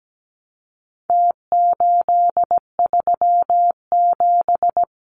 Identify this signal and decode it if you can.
Morse code, heard as a keyed tone, about 17 wpm, 700 Hz. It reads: T837
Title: Numbers - Contest Style - 4-Digit 17wpm